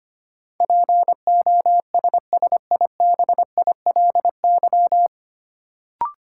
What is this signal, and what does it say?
Morse code, heard as a keyed tone, about 25 wpm, 700 Hz. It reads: POSSIBILY
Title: Sentences from Top 1300 Words - New Word Drill - 25wpm